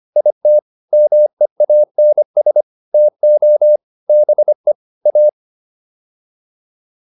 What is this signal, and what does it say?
Morse code, heard as a keyed tone, about 25 wpm, 600 Hz. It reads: IT MEANS TO BE A